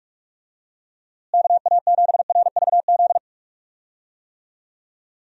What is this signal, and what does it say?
Morse code, heard as a keyed tone, about 46 wpm, 700 Hz. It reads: KA7RVZ
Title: USA Call Signs - 6 Characters - Repeated 2x - 45wpm